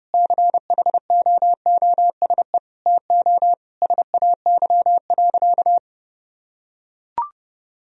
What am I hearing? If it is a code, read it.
Morse code, heard as a keyed tone, about 30 wpm, 700 Hz. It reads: CHOOSE TO SAY.
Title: Sentences from Top 1000 Words - Review Part 4 - 30wpm